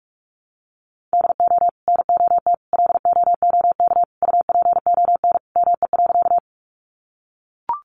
Morse code, heard as a keyed tone, about 44 wpm, 700 Hz. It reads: DO NOT LOOK UPON ME.
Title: Sentences from Top 1100 Words 45wpm